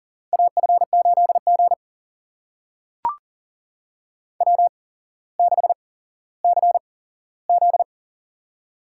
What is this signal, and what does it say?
Morse code, heard as a keyed tone, about 39 wpm, spaced as for 17 wpm, 700 Hz. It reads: AF8G W6CZ
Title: USA Call Signs - 4 Characters - 8x Character Spacing - 40wpm